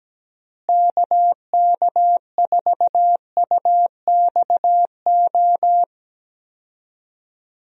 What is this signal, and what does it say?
Morse code, heard as a keyed tone, about 17 wpm, 700 Hz. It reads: KK4UXO
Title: USA Call Signs - 6 Characters - Repeated 3x - 17wpm